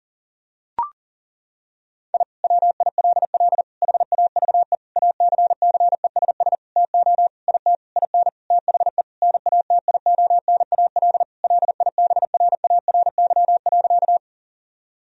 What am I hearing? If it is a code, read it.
Morse code, heard as a keyed tone, about 37 wpm, 700 Hz. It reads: I WILL HAVE ACCESS TO IT IN THE NATIONAL LIBRARY.
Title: Sentences from Top 1000 Words - Review Part 4 - 40wpm